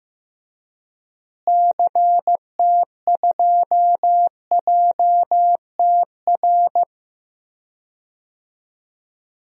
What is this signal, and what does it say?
Morse code, heard as a keyed tone, about 15 wpm, 700 Hz. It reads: CT2JTR
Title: DXCC-100 Call Signs - 6 Characters - Repeated 3x - 15wpm